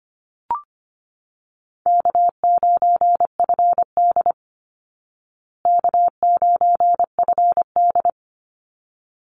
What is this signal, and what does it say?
Morse code, heard as a keyed tone, about 25 wpm, 700 Hz. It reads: K9FD K9FD
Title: Easy Call Signs Repeated 2x - 25wpm